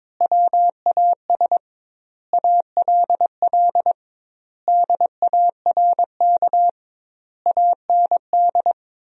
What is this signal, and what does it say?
Morse code, heard as a keyed tone, about 22 wpm, 700 Hz. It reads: WAS ALL DARK AND
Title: 2x Word-Spacing - Sentences from Top 500 Words - 22wpm